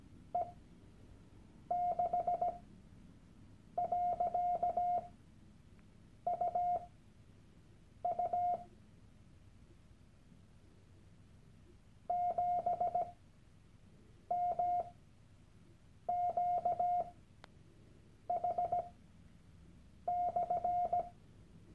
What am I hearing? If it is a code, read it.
Morse code, heard as a keyed tone, about 17 wpm, spaced as for 6 wpm, 710 Hz. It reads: T6.UU 7MQH/
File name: Workshop Dateien / CW-Online-Kurs / Lektion 001